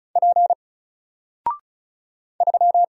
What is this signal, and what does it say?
Morse code, heard as a keyed tone, about 36 wpm, 700 Hz. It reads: P 3
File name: Single-Letter-Number-35wpm